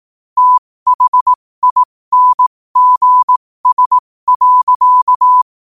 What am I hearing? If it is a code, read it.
Morse code, heard as a keyed tone, about 18 wpm, spaced as for 15 wpm, 1000 Hz. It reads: THINGS.
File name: qotdfiles / 15wpm